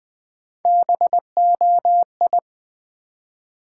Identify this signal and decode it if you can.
Morse code, heard as a keyed tone, about 20 wpm, 700 Hz. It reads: BOI
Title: Build A Word - 1 Letter At A Time - 4-Letter Words - 20wpm